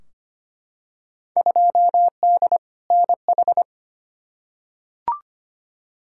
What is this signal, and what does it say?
Morse code, heard as a keyed tone, about 25 wpm, 700 Hz. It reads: 2D NH
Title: Field Day Exchanges Encore - 25wpm